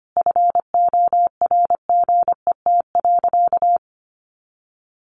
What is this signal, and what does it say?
Morse code, heard as a keyed tone, about 25 wpm, 700 Hz. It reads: FORGET.
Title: Sentences from Top 1500 Words 25wpm